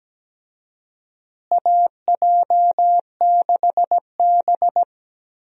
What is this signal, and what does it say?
Morse code, heard as a keyed tone, about 17 wpm, 700 Hz. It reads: AJ6B
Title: USA Call Signs - 4 Characters - Repeated 3x - 17wpm